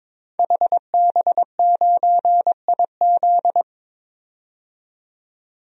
Morse code, heard as a keyed tone, about 22 wpm, 700 Hz. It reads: HB9IZ
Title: DXCC-100 Call Signs - 5 Characters - 22wpm